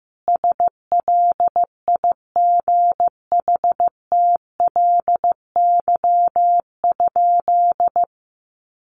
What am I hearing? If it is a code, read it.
Morse code, heard as a keyed tone, about 15 wpm, 700 Hz. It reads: SLIGHTLY?
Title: Sentences from Top 1300 Words 15wpm